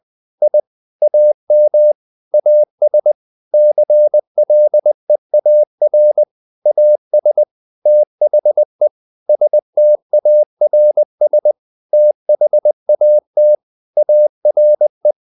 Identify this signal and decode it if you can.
Morse code, heard as a keyed tone, about 20 wpm, 600 Hz. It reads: I AM AS CLEAR AS THE STARS THAT ARE
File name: the old man and the sea part 2 20wpm